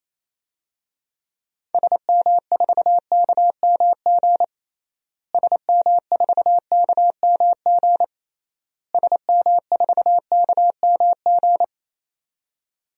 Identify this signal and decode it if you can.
Morse code, heard as a keyed tone, about 28 wpm, 700 Hz. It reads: SM4KMG SM4KMG SM4KMG
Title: DXCC-100 Call Signs - 6 Characters - Repeated 3x - 28wpm